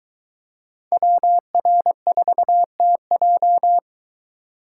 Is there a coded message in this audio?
Yes: Morse code from 0.9 to 3.8 s, heard as a keyed tone, about 21 wpm, 700 Hz. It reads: WR4TJ